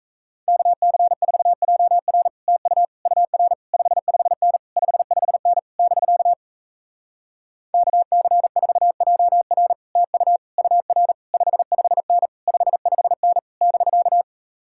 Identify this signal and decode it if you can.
Morse code, heard as a keyed tone, about 37 wpm, 700 Hz. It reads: KC4JR TU UR 55N 55N <BK> KC4JR TU UR 55N 55N <BK>